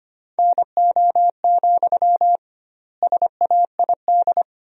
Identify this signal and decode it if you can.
Morse code, heard as a keyed tone, about 25 wpm, 700 Hz. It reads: NO, SAID
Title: 2x Word-Spacing - Sentences from Top 300 Words - 25wpm